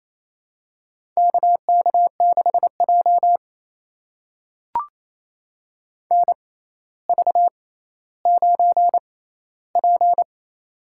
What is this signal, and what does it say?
Morse code, heard as a keyed tone, about 27 wpm, spaced as for 14 wpm, 700 Hz. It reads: KK6J NV9P
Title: USA Call Signs - 6x Character Spacing - 28wpm